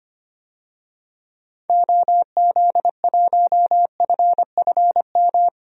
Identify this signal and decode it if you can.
Morse code, heard as a keyed tone, about 25 wpm, 700 Hz. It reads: OZ1FFM
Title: DXCC-100 Call Signs - 6 Characters - 25wpm